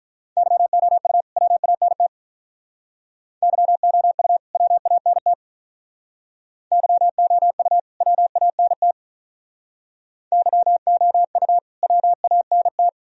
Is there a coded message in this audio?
Yes: Morse code from 0.4 to 12.9 s, heard as a keyed tone, about 46 wpm, 700 Hz. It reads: YOU WANT YOU WANT YOU WANT YOU WANT